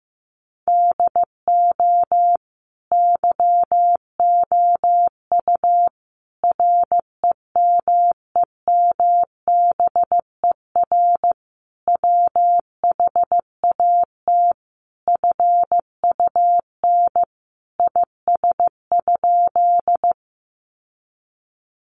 Morse code, heard as a keyed tone, about 15 wpm, 700 Hz. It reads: DO YOU REMEMBER WHAT FUN IS?
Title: Sentences from Top 700 Words 15wpm